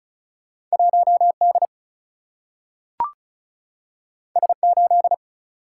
Morse code, heard as a keyed tone, about 36 wpm, 700 Hz. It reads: 1 D S 8
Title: Single-Letter Single-Number 35wpm